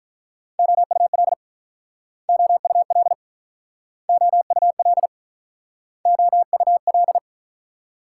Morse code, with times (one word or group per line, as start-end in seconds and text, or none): 0.6-1.3 OUL
2.3-3.1 OUL
4.1-5.1 OUL
6.1-7.2 OUL